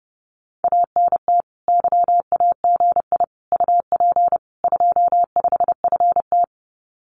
Morse code, heard as a keyed tone, about 30 wpm, 700 Hz. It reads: ANT YAGI UP 25FT
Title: Common QSO Phrases Encore 30wpm